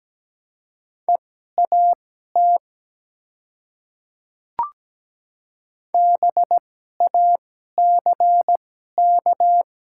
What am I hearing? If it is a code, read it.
Morse code, heard as a keyed tone, about 17 wpm, spaced as for 12 wpm, 690 Hz. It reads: EAT BACK
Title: 2x Character Spacing - Top 300 Words - 17wpm